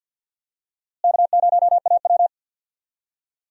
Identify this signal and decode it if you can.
Morse code, heard as a keyed tone, about 46 wpm, 700 Hz. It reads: K0AW
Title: USA Call Signs - 4 Characters - Repeated 3x - 50wpm